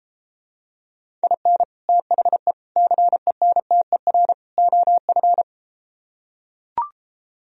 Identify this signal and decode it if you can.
Morse code, heard as a keyed tone, about 33 wpm, 700 Hz. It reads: IN THE CENTER OF